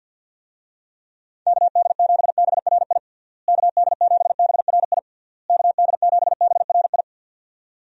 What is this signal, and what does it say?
Morse code, heard as a keyed tone, about 52 wpm, 700 Hz. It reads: KD7BRI KD7BRI KD7BRI
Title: USA Call Signs - 6 Characters - Repeated 3x - 50wpm